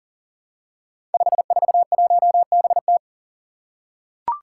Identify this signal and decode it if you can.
Morse code, heard as a keyed tone, about 40 wpm, 700 Hz. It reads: 541BT
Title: Five Challenging Letter-Number Combinations 40wpm